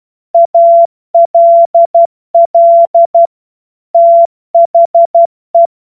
Morse code, heard as a keyed tone, about 12 wpm, 680 Hz. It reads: ALL THE
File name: Morsekurs / Witze